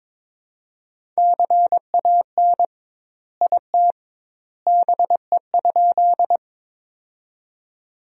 Morse code, heard as a keyed tone, about 22 wpm, 700 Hz. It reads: CAN IT BE?
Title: 2x Word-Spacing - Sentences from Top 100 Words - 22wpm